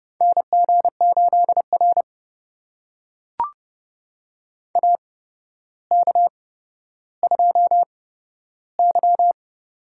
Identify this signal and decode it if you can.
Morse code, heard as a keyed tone, about 30 wpm, spaced as for 13 wpm, 700 Hz. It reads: NG8R AK2Y
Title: USA Call Signs - 4 Characters - 8x Character Spacing - 30wpm